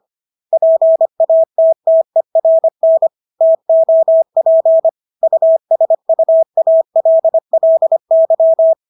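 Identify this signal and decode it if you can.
Morse code, heard as a keyed tone, about 25 wpm, 650 Hz. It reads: PATTERN TOP USUALLY
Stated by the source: Morse practice of English words at 25 wpm from CW med Gnister.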